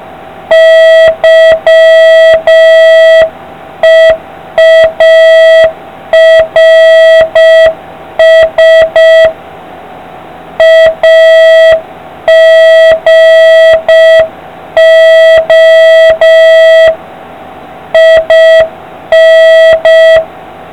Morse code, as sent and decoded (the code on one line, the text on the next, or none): -.-- . .- .-. ... / .- --. --- / .. -.
YEARS AGO IN